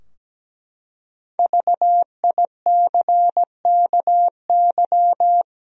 Morse code, heard as a keyed tone, about 17 wpm, 700 Hz. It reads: VICKY
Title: USA Common Names - Max 5 Letters - 17wpm